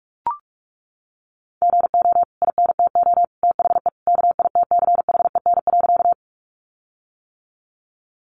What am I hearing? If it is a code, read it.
Morse code, heard as a keyed tone, about 44 wpm, 700 Hz. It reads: GO INTO THE KITCHEN.
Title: Sentences from Top 1400 Words 45wpm